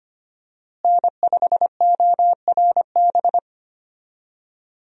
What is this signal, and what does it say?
Morse code, heard as a keyed tone, about 25 wpm, 700 Hz. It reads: N5ORB
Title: USA Call Signs - 5 Characters - Repeated 2x - 25wpm